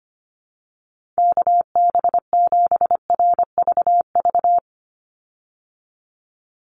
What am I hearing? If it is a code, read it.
Morse code, heard as a keyed tone, about 25 wpm, 700 Hz. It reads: KB7RVV
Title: USA Call Signs 25wpm